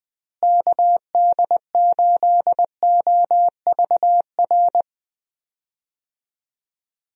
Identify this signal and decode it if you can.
Morse code, heard as a keyed tone, about 20 wpm, 700 Hz. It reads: KD8OVR
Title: USA Call Signs - 6 Characters - Repeated 2x - 20wpm